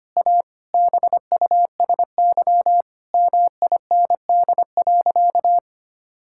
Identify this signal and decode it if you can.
Morse code, heard as a keyed tone, about 25 wpm, 700 Hz. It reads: A BUSY MIND.
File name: practice / 25wpm / Difficult Sentences with Hints - 25wpm